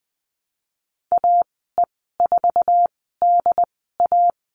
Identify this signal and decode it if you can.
Morse code, heard as a keyed tone, about 20 wpm, spaced as for 14 wpm, 700 Hz. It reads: AE4DA
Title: 069 Lesson 12g - Letter d - Call Signs 20wpm